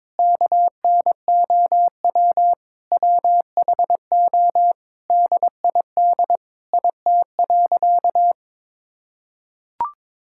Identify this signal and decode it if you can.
Morse code, heard as a keyed tone, about 22 wpm, 700 Hz. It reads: KNOW WHO DID IT.
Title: Sentences from Top 1000 Words - Review Part 6 - 22wpm